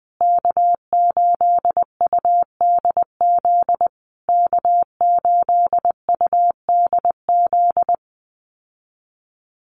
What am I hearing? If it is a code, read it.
Morse code, heard as a keyed tone, about 20 wpm, 700 Hz. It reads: K8UDZ K8UDZ
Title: USA Call Signs Repeated 3x 20wpm